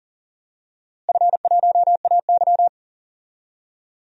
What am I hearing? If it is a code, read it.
Morse code, heard as a keyed tone, about 40 wpm, 700 Hz. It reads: F1AY
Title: DXCC-100 Call Signs - 4 Characters - Repeated 3x - 40wpm